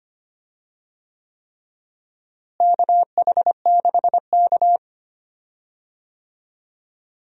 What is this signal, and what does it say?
Morse code, heard as a keyed tone, about 25 wpm, 700 Hz. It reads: KH6K
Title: DXCC-100 Call Signs - 4 Characters - Repeated 3x - 25wpm